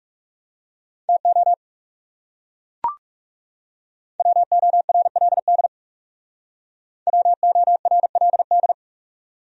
Morse code, heard as a keyed tone, about 37 wpm, 700 Hz. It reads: TO WORLD WORLD
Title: Speed-Racing Top 200 Words 30wpm